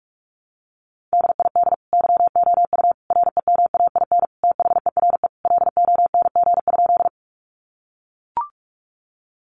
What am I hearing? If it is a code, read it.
Morse code, heard as a keyed tone, about 44 wpm, 700 Hz. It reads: DID YOU REMAIN THERE LONG?
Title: Sentences from Top 1200 Words 45wpm